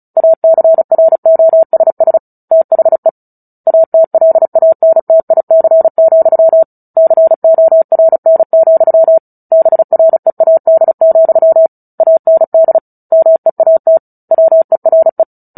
Morse code, heard as a keyed tone, about 34 wpm, 650 Hz. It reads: ACROSS THE ATLANTIC, CORN, BREAD, AND MEAT WERE
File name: Book Two / 09-Chapter 9 - 35 WPM